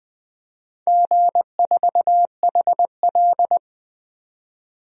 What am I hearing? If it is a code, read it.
Morse code, heard as a keyed tone, about 20 wpm, 700 Hz. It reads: G4HL
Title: DXCC-100 Call Signs - 4 Characters - 20wpm